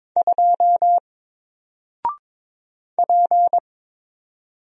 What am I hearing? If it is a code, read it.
Morse code, heard as a keyed tone, about 22 wpm, 700 Hz. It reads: 2 P
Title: Single Character (Letter:Number:Punctuation:Prosign) Alt 22wpm